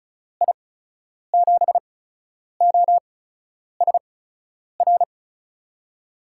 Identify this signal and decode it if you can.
Morse code, heard as a keyed tone, about 36 wpm, spaced as for 10 wpm, 700 Hz. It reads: I7OSR
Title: USA Call Signs - 6 Characters - 8x Character Spacing - 35wpm